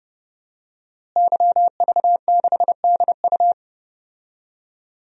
Morse code, heard as a keyed tone, about 29 wpm, 700 Hz. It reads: YV6DU
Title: DXCC-100 Call Signs - 5 Characters - Repeated 3x - 30wpm